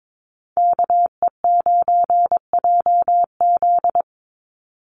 Morse code, heard as a keyed tone, about 22 wpm, 700 Hz. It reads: KE9JZ
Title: USA Call Signs 22wpm